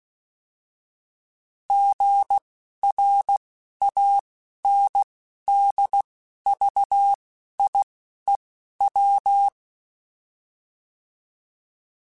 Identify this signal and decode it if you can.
Morse code, heard as a keyed tone, about 16 wpm, spaced as for 11 wpm, 800 Hz. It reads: GRANDVIEW
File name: fromSMs / FromSMs mp3s / Cities@10